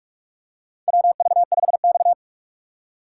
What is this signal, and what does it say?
Morse code, heard as a keyed tone, about 47 wpm, 700 Hz. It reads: WV5X
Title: USA Call Signs - 4 Characters - 6x Character Spacing - 45wpm